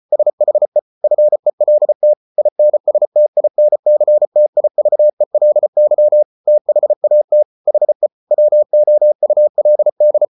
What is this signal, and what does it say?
Morse code, heard as a keyed tone, about 34 wpm, 600 Hz. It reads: SHE FELT INSTINCTIVELY THAT HE WOULD